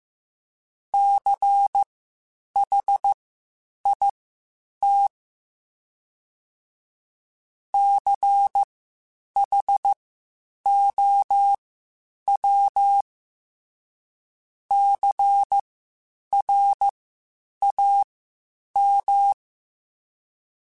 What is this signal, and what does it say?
Morse code, heard as a keyed tone, about 15 wpm, spaced as for 8 wpm, 800 Hz. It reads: CHIT CHOW CRAM
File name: Spaced Alphas / Spaced Alphas mp3s / Cs sp @13